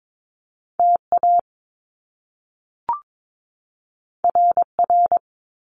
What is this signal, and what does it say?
Morse code, heard as a keyed tone, about 22 wpm, 700 Hz. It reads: TA RR
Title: Top Two Letter Combinations in Words 22wpm